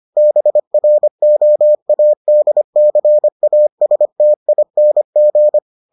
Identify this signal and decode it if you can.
Morse code, heard as a keyed tone, about 25 wpm, 600 Hz. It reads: BROADCASTING